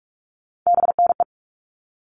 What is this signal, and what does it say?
Morse code, heard as a keyed tone, about 44 wpm, 700 Hz. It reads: BNE
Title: All Cut Numbers - 3-Digit 45wpm